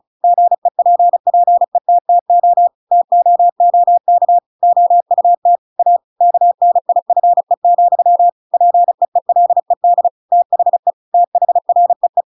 Morse code, heard as a keyed tone, about 34 wpm, 700 Hz. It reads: GEPPETTO TOOK OUT A KNIFE, PEELED THE THREE
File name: Pinocchio / Pinocchio ch7 35 35 ik1hge